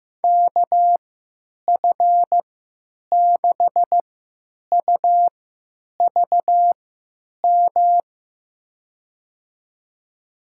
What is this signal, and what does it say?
Morse code, heard as a keyed tone, about 15 wpm, spaced as for 9 wpm, 700 Hz. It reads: KF6UVM
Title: USA Call Signs - 6 Characters - 3x Character Spacing - 15wpm